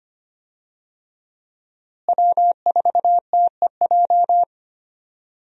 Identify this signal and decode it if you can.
Morse code, heard as a keyed tone, about 25 wpm, 700 Hz. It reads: W4TEJ
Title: USA Call Signs - 5 Characters - Repeated 3x - 25wpm